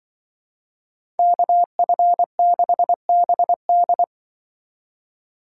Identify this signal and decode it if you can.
Morse code, heard as a keyed tone, about 25 wpm, 700 Hz. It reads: KF6BD
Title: Speed Racing - USA 2x2 Call Signs 20wpm